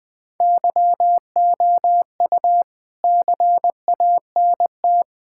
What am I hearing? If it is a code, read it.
Morse code, heard as a keyed tone, about 20 wpm, 700 Hz. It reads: YOU CANT
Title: Sentences with 5-Word Phrases - Spoken Beginning - 20wpm